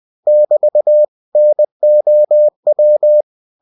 Morse code, heard as a keyed tone, about 20 wpm, 600 Hz. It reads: <BT> NOW
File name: cw-books / winnie-the-pooh / winnie-20wpm-page-0002